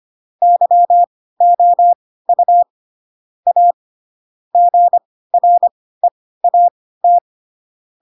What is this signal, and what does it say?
Morse code, heard as a keyed tone, about 25 wpm, spaced as for 16 wpm, 700 Hz. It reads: YOU A GREAT
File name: Pinocchio / Pinocchio ch4 10 25 ik1hge